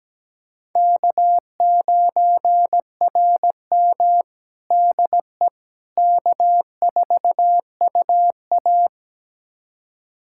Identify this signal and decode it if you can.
Morse code, heard as a keyed tone, about 17 wpm, 700 Hz. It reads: K9RM DE K4UA
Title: USA Callsign DE Callsign - 4 Characters - 17wpm